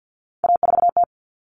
Morse code, heard as a keyed tone, about 50 wpm, 700 Hz. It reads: A4T
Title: All Cut Numbers - 3-Digit 50wpm